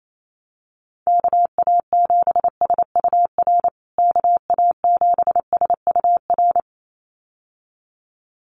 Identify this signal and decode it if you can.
Morse code, heard as a keyed tone, about 28 wpm, 700 Hz. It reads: KA7SUR KA7SUR
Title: USA Call Signs Repeated 2x 28wpm